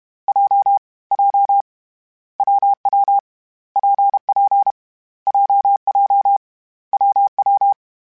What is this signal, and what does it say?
Morse code, heard as a keyed tone, about 32 wpm, 800 Hz. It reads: J J WW PP JJ WW
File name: Difficult Combos / Difficult Combos-mp3s / WPJ WPJ1 @30